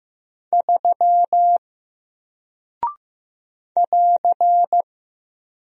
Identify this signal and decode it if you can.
Morse code, heard as a keyed tone, about 15 wpm, 700 Hz. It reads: 3 <AR>
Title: Single Character (Letter:Number:Punctuation:Prosign) Alt 15wpm